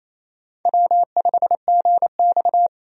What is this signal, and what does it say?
Morse code, heard as a keyed tone, about 28 wpm, 700 Hz. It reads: W5GX
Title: USA Call Signs - 4 Characters - 28wpm